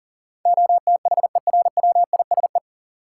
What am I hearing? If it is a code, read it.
Morse code, heard as a keyed tone, about 37 wpm, 700 Hz. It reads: OTHERWISE
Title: Sentences from Top 1200 Words - New Word Drill Alt - 40wpm